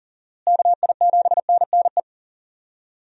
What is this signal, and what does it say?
Morse code, heard as a keyed tone, about 40 wpm, 700 Hz. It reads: KI7NNE
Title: USA Call Signs - 6 Characters - Repeated 2x - 40wpm